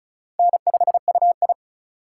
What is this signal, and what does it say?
Morse code, heard as a keyed tone, about 36 wpm, 700 Hz. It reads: N5UI
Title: K1USN SST QSO Practice - 35wpm